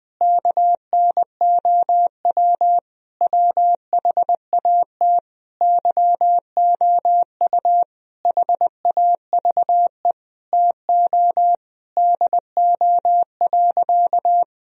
Morse code, heard as a keyed tone, about 20 wpm, 700 Hz. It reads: KNOW WHAT YOU HAVE TO DO.
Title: Sentences from Top 1000 Words - Review Part 3 - 20wpm